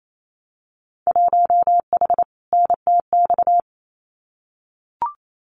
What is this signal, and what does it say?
Morse code, heard as a keyed tone, about 28 wpm, 700 Hz. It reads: 1H NTX
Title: Winter Field Day Exchanges 28wpm